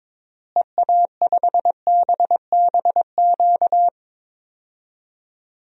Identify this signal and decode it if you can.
Morse code, heard as a keyed tone, about 22 wpm, 700 Hz. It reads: EA5BBQ
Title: DXCC-100 Call Signs - 6 Characters - Repeated 2x - 22wpm